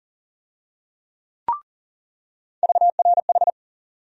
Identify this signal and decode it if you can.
Morse code, heard as a keyed tone, about 40 wpm, 700 Hz. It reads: VRH
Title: USA Call Sign Postfixes - 3 Letters 40wpm